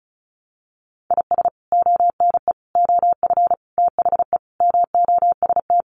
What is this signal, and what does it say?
Morse code, heard as a keyed tone, about 34 wpm, 700 Hz. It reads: IS ONE OF THE MOST
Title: Sets of 5 Words 35wpm